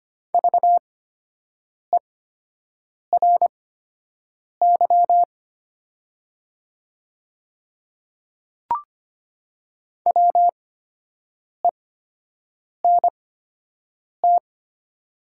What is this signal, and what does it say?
Morse code, heard as a keyed tone, about 25 wpm, spaced as for 7 wpm, 700 Hz. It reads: VERY WENT
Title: 8x Character Spacing - Top 200 Words - 25wpm